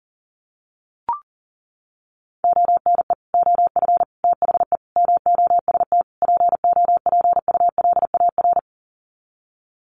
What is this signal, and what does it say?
Morse code, heard as a keyed tone, about 40 wpm, 700 Hz. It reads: ONE OF THE MOST POPULAR
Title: Sets of 5 Words 40wpm